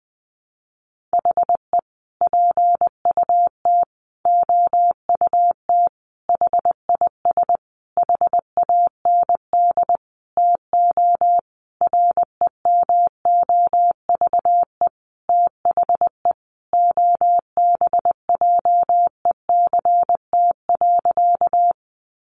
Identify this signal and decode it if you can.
Morse code, heard as a keyed tone, about 20 wpm, 700 Hz. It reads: HE PUT OUT HIS HAND TO REMOVE THE OBJECT.
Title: Sentences from Top 800 Words 20wpm